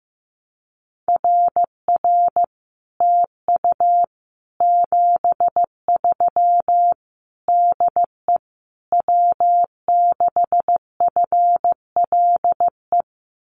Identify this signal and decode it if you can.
Morse code, heard as a keyed tone, about 15 wpm, 700 Hz. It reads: RR TU 73 DE W6FLE
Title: Common QSO Phrases Encore 15wpm